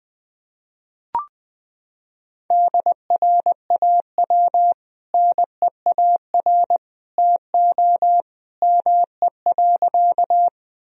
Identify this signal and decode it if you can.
Morse code, heard as a keyed tone, about 20 wpm, 700 Hz. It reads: DRAW NEAR TO ME.
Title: Sentences from Top 1000 Words - Review Part 1 - 20wpm